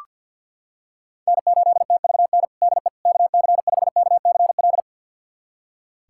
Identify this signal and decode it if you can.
Morse code, heard as a keyed tone, about 46 wpm, 700 Hz. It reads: N8TVN DE KC5KCL
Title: USA Callsign DE Callsign 50wpm